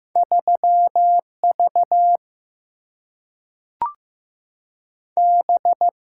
Morse code, heard as a keyed tone, about 15 wpm, 700 Hz. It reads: 3 V B
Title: Single-Letter Single-Number 15wpm